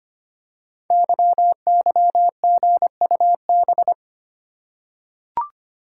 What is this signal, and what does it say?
Morse code, heard as a keyed tone, about 25 wpm, 700 Hz. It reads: YYGUB
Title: Code Groups - 5 Letters - 25wpm